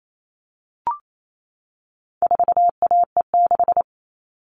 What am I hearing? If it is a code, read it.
Morse code, heard as a keyed tone, about 28 wpm, 700 Hz. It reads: 4AE6
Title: All Cut Numbers - 4-Digit 28wpm